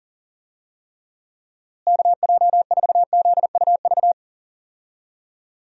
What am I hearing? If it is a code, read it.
Morse code, heard as a keyed tone, about 40 wpm, 700 Hz. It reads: KJ4ZUV
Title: USA Call Signs - 6 Characters - Repeated 2x - 40wpm